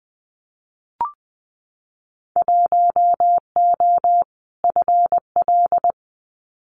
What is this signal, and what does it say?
Morse code, heard as a keyed tone, about 20 wpm, 700 Hz. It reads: 1O FL
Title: Winter Field Day Exchanges 20wpm